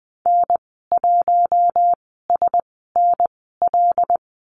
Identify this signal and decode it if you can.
Morse code, heard as a keyed tone, about 20 wpm, spaced as for 14 wpm, 700 Hz. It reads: N1SNL
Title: 082 Lesson 14g - Number 2 - Call Signs 20wpm